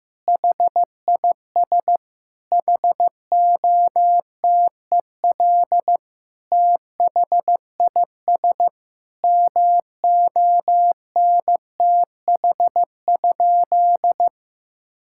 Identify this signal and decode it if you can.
Morse code, heard as a keyed tone, about 15 wpm, 700 Hz. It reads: HIS HOTEL THIS MONTH?
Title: Sentences from Top 1600 Words 15wpm